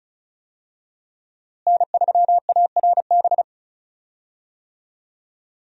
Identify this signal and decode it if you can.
Morse code, heard as a keyed tone, about 34 wpm, 700 Hz. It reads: N3ARB
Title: USA Call Signs - 5 Characters - Repeated 2x - 35wpm